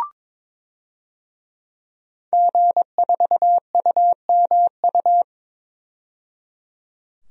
Morse code, heard as a keyed tone, about 22 wpm, 700 Hz. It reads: G4UMU
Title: DXCC-100 Call Signs - 5 Characters - 22wpm